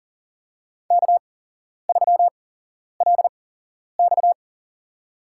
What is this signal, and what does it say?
Morse code, heard as a keyed tone, about 40 wpm, spaced as for 11 wpm, 700 Hz. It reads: K3LX
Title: USA Call Signs - 4 Characters - 8x Character Spacing - 40wpm